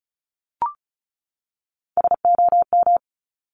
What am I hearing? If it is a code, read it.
Morse code, heard as a keyed tone, about 36 wpm, 700 Hz. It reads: SOM
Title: Top Three Letter Combinations in Words 35wpm